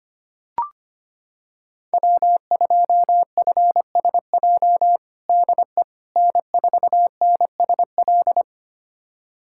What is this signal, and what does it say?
Morse code, heard as a keyed tone, about 25 wpm, 700 Hz. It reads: W2FSJ DE N4NSL
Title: USA Callsign DE Callsign - 5 Characters - 25wpm